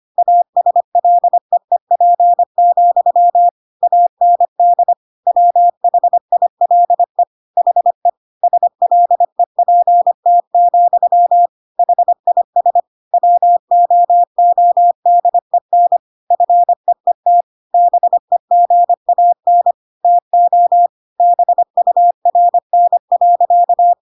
Morse code, heard as a keyed tone, about 25 wpm, 700 Hz. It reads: ASLEEP, AND WHILE HE SLEPT, HIS WOODEN FEET BEGAN TO BURN.